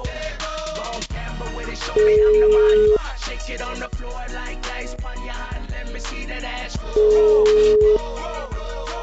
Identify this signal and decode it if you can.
Ringback tone (Europe, 425 Hz).